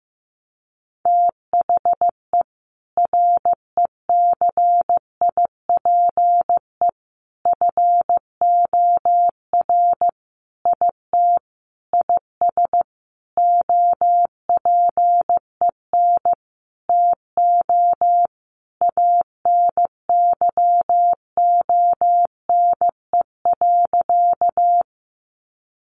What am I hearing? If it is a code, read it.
Morse code, heard as a keyed tone, about 15 wpm, 700 Hz. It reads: THE RECIPE FOR IT IS OPEN TO ANYONE.
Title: Sentences from Top 1200 Words 15wpm